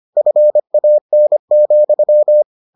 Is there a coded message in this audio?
Yes: Morse code from 0.2 to 2.4 s, heard as a keyed tone, about 25 wpm, 600 Hz. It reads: FAN,